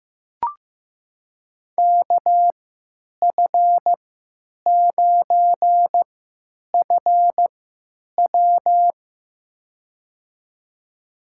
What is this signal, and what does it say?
Morse code, heard as a keyed tone, about 15 wpm, spaced as for 9 wpm, 700 Hz. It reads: KF9FW
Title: USA Call Signs - 5 Characters - 3x Character Spacing - 15wpm